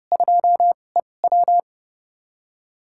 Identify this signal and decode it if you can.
Morse code, heard as a keyed tone, about 30 wpm, spaced as for 22 wpm, 700 Hz. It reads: 2EW
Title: USA Call Signs - 2x Character Spacing - 30wpm